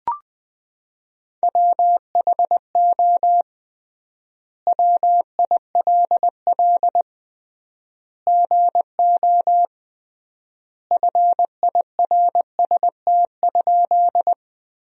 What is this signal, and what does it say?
Morse code, heard as a keyed tone, about 20 wpm, 700 Hz. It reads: WHO WILL GO FIRST?
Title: 3x Word-Spacing - Sentences from Top 100 Words - 20wpm